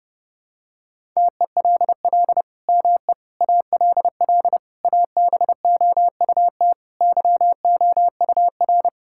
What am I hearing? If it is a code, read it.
Morse code, heard as a keyed tone, about 30 wpm, 700 Hz. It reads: TELL ME ALL ABOUT YOUR
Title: Sentences from Top 1000 Words - Review Part 5 - 30wpm